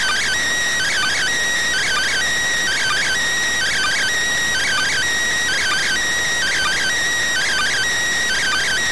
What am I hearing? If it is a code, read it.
Two-tone teleprinter signal, about 400 Hz shift.